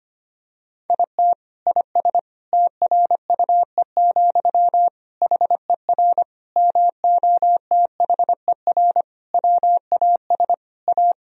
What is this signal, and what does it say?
Morse code, heard as a keyed tone, about 25 wpm, 700 Hz. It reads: IT IS TRUE, HER MOTHER WAS A
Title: Sentences from Top 1000 Words - Review Part 2 - 25wpm